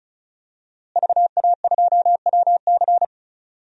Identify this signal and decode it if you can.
Morse code, heard as a keyed tone, about 36 wpm, 700 Hz. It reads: VA2WC
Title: DXCC-100 Call Signs - 5 Characters - Repeated 3x - 35wpm